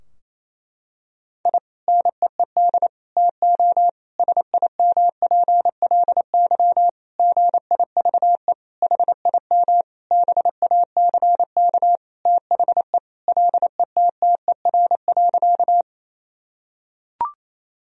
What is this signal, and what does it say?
Morse code, heard as a keyed tone, about 28 wpm, 700 Hz. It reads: I NEED TO SIMPLY GIVE HIM BACK THE LETTER.
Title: Sentences from Top 1000 Words - Review Part 4 - 28wpm